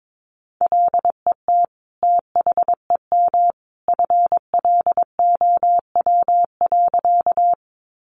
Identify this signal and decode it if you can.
Morse code, heard as a keyed tone, about 22 wpm, 700 Hz. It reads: LET THEM FLOW.
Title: Sentences from Top 1200 Words 22wpm